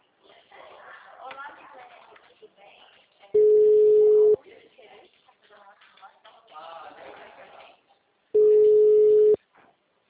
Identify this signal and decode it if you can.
Ringback tone (Europe, 425 Hz), with speech.
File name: out- -2001- - - .2